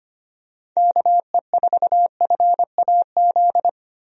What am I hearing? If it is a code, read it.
Morse code, heard as a keyed tone, about 25 wpm, 700 Hz. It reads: KE4FAZ
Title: USA Call Signs - 6 Characters - 25wpm